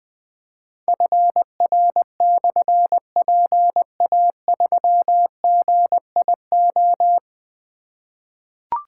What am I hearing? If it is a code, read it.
Morse code, heard as a keyed tone, about 20 wpm, 700 Hz. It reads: FR/PA3GIO
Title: Difficult Call Signs 20wpm